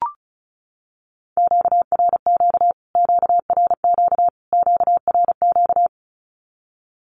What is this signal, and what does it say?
Morse code, heard as a keyed tone, about 34 wpm, 700 Hz. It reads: QRQ QRQ QRQ
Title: Q-Codes Repeated 3x 35wpm